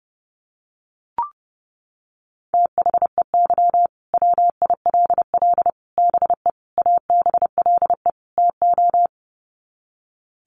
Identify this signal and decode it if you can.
Morse code, heard as a keyed tone, about 30 wpm, 700 Hz. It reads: THEY WILL BE ABLE TO
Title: Sets of 5 Words Encore 30wpm